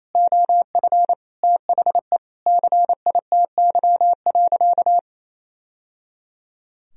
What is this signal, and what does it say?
Morse code, heard as a keyed tone, about 27 wpm, 700 Hz. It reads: OF THE CITY.
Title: Sentences from Top 1000 Words - Review Part 2 - 28wpm